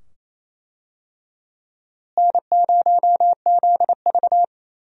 Morse code, heard as a keyed tone, about 28 wpm, 700 Hz. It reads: N0ZV
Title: USA Call Signs - 4 Characters - Repeated 2x - 28wpm